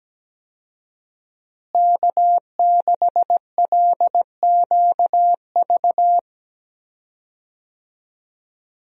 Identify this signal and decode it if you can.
Morse code, heard as a keyed tone, about 17 wpm, 700 Hz. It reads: K6LQV
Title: USA Call Signs - 5 Characters - Repeated 2x - 17wpm